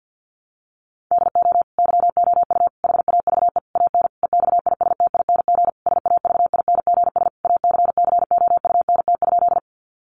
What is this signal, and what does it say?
Morse code, heard as a keyed tone, about 50 wpm, 700 Hz. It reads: DO YOU HAVE AN EXISTING SAVINGS ACCOUNT?
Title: Sentences from Top 1500 Words 50wpm